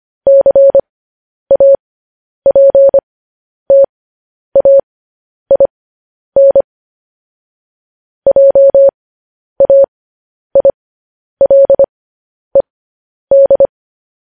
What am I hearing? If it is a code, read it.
Morse code, heard as a keyed tone, about 25 wpm, spaced as for 10 wpm, 550 Hz. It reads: CAPTAIN JAILED